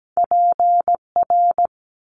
Morse code, heard as a keyed tone, about 17 wpm, 700 Hz. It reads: PR
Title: Top Two Letter Combinations in Words 17wpm